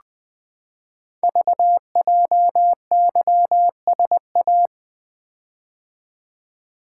Morse code, heard as a keyed tone, about 20 wpm, 700 Hz. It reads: VJYSA
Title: Code Groups - 5 Letters - 20wpm